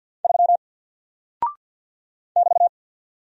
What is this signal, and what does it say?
Morse code, heard as a keyed tone, about 48 wpm, 700 Hz. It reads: 3 <BT>
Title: Single Character (Letter:Number:Punctuation:Prosign) Alt 50wpm